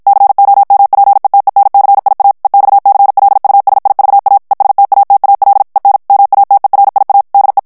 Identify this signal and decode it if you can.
Morse code, heard as a keyed tone, about 55 wpm, 800 Hz. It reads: COMPETENCIA EXCLUSIVA ESTATAL EN MATERIA DE
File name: morse mp3 / ESPAGNOL TEXTE EN CLAIR 2 VITESSE 50 MOTS MINUTES